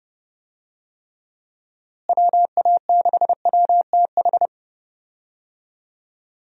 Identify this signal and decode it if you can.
Morse code, heard as a keyed tone, about 32 wpm, 700 Hz. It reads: WA6WTH
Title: USA Call Signs - 6 Characters - Repeated 3x - 30wpm